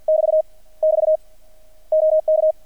Morse code, heard as a keyed tone, about 50 wpm, 630 Hz. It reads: <BT> <BT> OK
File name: archived / CRSnet